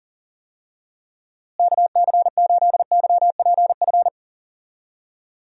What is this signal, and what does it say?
Morse code, heard as a keyed tone, about 40 wpm, 700 Hz. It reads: KC8YPF
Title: USA Call Signs - 6 Characters - Repeated 3x - 40wpm